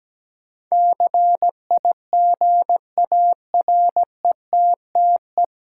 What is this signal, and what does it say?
Morse code, heard as a keyed tone, about 17 wpm, 700 Hz. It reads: CIGARETTE
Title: Sentences from Top 1000 Words - New Word Drill - 17wpm